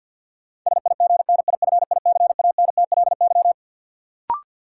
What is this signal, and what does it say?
Morse code, heard as a keyed tone, about 47 wpm, 700 Hz. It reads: SIGNIFICANTLY
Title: Sentences from Top 1500 Words - New Word Drill - 50wpm